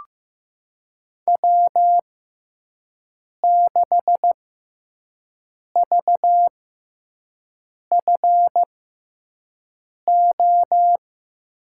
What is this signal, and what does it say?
Morse code, heard as a keyed tone, about 15 wpm, spaced as for 5 wpm, 700 Hz. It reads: W6VFO
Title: USA Call Signs - 6x Character Spacing - 15wpm